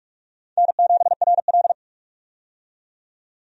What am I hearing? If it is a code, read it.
Morse code, heard as a keyed tone, about 46 wpm, 700 Hz. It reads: N7RL
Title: USA Call Signs - 4 Characters - 45wpm